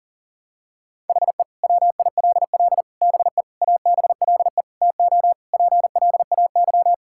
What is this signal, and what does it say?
Morse code, heard as a keyed tone, about 40 wpm, 700 Hz. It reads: HE WILL BE ABLE TO PLAY
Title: Sentences from Top 1600 Words 40wpm